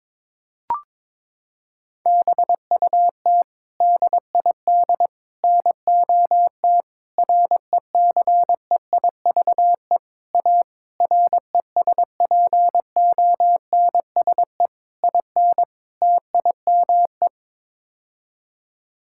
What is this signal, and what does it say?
Morse code, heard as a keyed tone, about 22 wpm, 700 Hz. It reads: BUT DID NOT RECEIVE A RESPONSE IN TIME
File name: practice / 22wpm / Sets of 8 Words - 22wpm